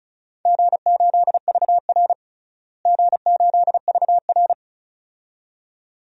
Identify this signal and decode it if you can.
Morse code, heard as a keyed tone, about 36 wpm, 700 Hz. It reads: G8VR G8VR
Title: DXCC-100 Call Signs - 4 Characters - Repeated 2x - 35wpm